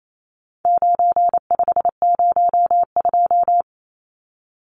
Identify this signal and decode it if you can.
Morse code, heard as a keyed tone, about 28 wpm, 700 Hz. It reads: 9502
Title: Numbers - 4-Digit 28wpm